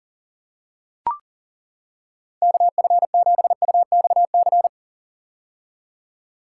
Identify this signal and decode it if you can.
Morse code, heard as a keyed tone, about 40 wpm, 700 Hz. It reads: KF7UXC
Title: K1USN SST QSO Practice - 40wpm